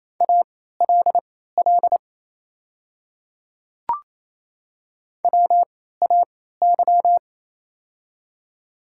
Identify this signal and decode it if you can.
Morse code, heard as a keyed tone, about 28 wpm, spaced as for 16 wpm, 700 Hz. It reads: ALL WAY
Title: 3x Character Spacing - Top 100 Words - 28wpm